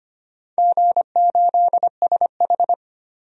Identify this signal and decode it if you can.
Morse code, heard as a keyed tone, about 25 wpm, 700 Hz. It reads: G8SH
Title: DXCC-100 Call Signs - 4 Characters - 25wpm